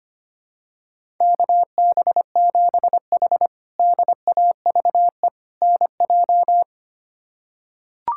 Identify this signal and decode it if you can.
Morse code, heard as a keyed tone, about 25 wpm, 700 Hz. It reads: KB7H DAVE NJ
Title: K1USN SST QSO Practice - 25wpm